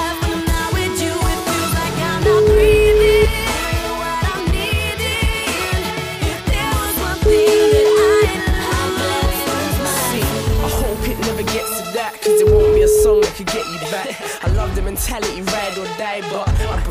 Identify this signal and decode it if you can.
Ringback tone (Europe, 425 Hz), with noise.